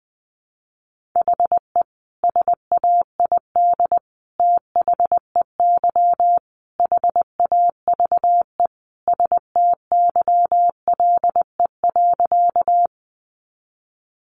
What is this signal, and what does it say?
Morse code, heard as a keyed tone, about 20 wpm, 700 Hz. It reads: HE SAID THEY HAVE STYLE.
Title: Sentences from Top 800 Words 20wpm